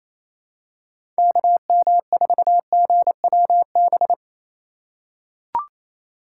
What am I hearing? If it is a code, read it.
Morse code, heard as a keyed tone, about 28 wpm, 700 Hz. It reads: KM4GWB
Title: USA Call Signs - 2x Character Spacing - 28wpm